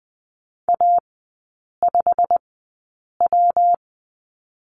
Morse code, heard as a keyed tone, about 20 wpm, spaced as for 8 wpm, 700 Hz. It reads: A5W
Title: 114 Lesson 19e - Letter w - Three Characters 20wpm